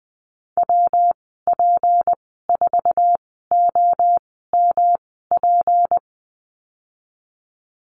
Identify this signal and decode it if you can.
Morse code, heard as a keyed tone, about 20 wpm, spaced as for 14 wpm, 700 Hz. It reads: WP4OMP
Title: 173 Lesson 28g - Number 9 - Call Signs 20wpm